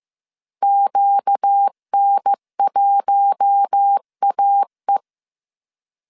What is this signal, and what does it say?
Morse code, heard as a keyed tone, about 15 wpm, 800 Hz. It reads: QN1AE